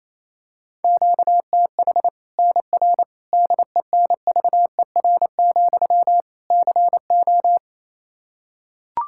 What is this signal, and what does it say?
Morse code, heard as a keyed tone, about 28 wpm, 700 Hz. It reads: QTH NR DENVER, CO
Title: Practice QSO - Exchange 1 - Beginning - 28wpm